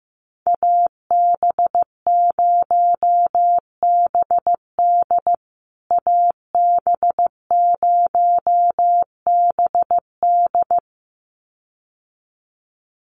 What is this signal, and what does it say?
Morse code, heard as a keyed tone, about 15 wpm, 700 Hz. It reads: AB0BD AB0BD
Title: USA Call Signs Repeated 2x 15wpm